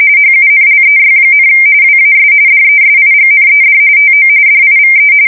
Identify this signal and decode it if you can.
RTTY, about 45.45 baud. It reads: HUNLEY, WAS DESIGNED TO DIVE
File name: fsk170 / usnavy3